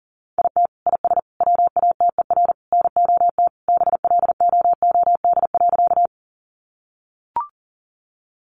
Morse code, heard as a keyed tone, about 40 wpm, 700 Hz. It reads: IT IS WATER NOT BLOOD.
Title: Sentences from Top 1100 Words 40wpm